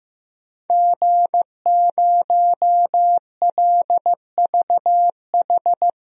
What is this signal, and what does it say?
Morse code, heard as a keyed tone, about 16 wpm, 700 Hz. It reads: G0LVH
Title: DXCC-100 Call Signs - 5 Characters - Repeated 2x - 15wpm